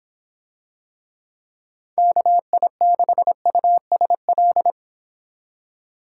Morse code, heard as a keyed tone, about 25 wpm, 700 Hz. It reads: KI6USL